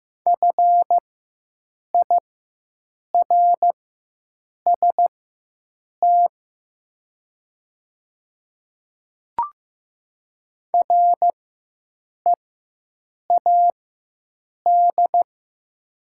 Morse code, heard as a keyed tone, about 15 wpm, spaced as for 7 wpm, 700 Hz. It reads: FIRST READ
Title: 4x Character Spacing - Top 200 Words - 15wpm